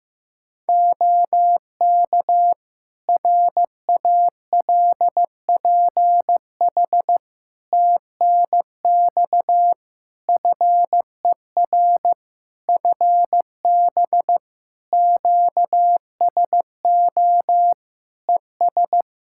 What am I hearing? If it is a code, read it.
Morse code, heard as a keyed tone, about 15 wpm, 700 Hz. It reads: OK RALPH TNX FER FB QSO ES
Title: Practice QSO - Exchange 4 - Ending - 15wpm